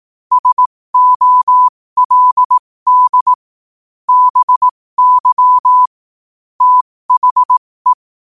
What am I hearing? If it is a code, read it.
Morse code, heard as a keyed tone, about 18 wpm, spaced as for 15 wpm, 1000 Hz. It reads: SOLD BY THE
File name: qotdfiles / 15wpm